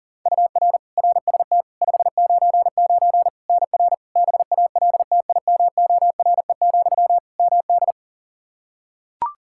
Morse code, heard as a keyed tone, about 40 wpm, 700 Hz. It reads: UR RST 599 NR BALTIMORE, MD
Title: Practice QSO - Exchange 1 - Beginning - 40wpm